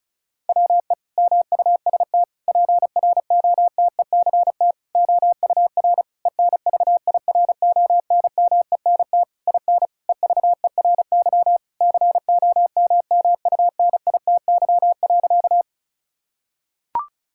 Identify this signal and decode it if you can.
Morse code, heard as a keyed tone, about 36 wpm, 700 Hz. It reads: WE MUST PROTECT OUR ENVIRONMENT IN EVERY COMMUNITY.
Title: Sentences from Top 1000 Words - Review Part 5 - 35wpm